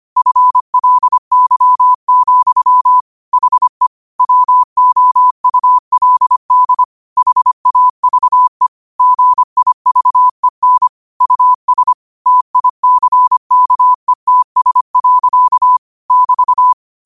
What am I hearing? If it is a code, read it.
Morse code, heard as a keyed tone, about 25 wpm, 1000 Hz. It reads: FLY, HE WOULD HAVE GIVEN US TICKETS. <BT>
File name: 25wpm